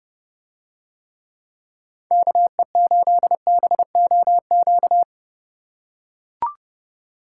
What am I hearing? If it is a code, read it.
Morse code, heard as a keyed tone, about 29 wpm, 700 Hz. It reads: KE8BOQ